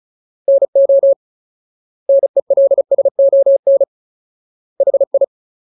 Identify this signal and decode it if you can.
Morse code, heard as a keyed tone, about 34 wpm, 550 Hz. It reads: NO NELSON HI
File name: Ultimate Word List / 35WPM Ultimate